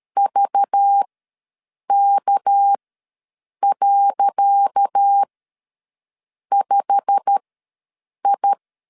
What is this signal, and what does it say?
Morse code, heard as a keyed tone, about 13 wpm, spaced as for 7 wpm, 800 Hz. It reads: VK.5I
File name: LEZIONI / 10 SIGNS / 10D COMA ws 12wpm 3618